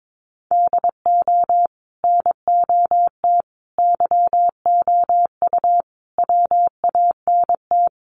Morse code, heard as a keyed tone, about 22 wpm, 700 Hz. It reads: DO NOT YOU WANT
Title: Sets of 5 Words Encore2 22wpm